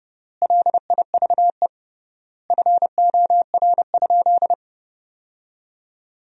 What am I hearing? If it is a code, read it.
Morse code, heard as a keyed tone, about 30 wpm, 700 Hz. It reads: LIVE FOR?
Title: 3x Word-Spacing - Sentences from Top 200 Words - 30wpm